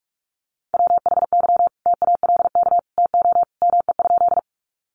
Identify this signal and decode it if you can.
Morse code, heard as a keyed tone, about 44 wpm, 700 Hz. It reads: WHY TALK TO ME?
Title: Sentences from Top 400 Words 45wpm